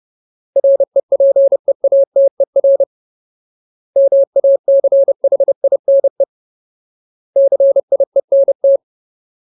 Morse code, heard as a keyed tone, about 30 wpm, 550 Hz. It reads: REPEATER MACHINE CIENT
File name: Ultimate Word List / 30WPM Ultimate